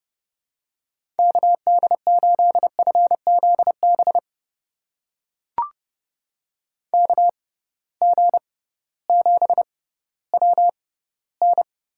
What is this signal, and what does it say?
Morse code, heard as a keyed tone, about 30 wpm, spaced as for 16 wpm, 700 Hz. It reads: KD8FZB KG7WN
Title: USA Call Signs - 6x Character Spacing - 30wpm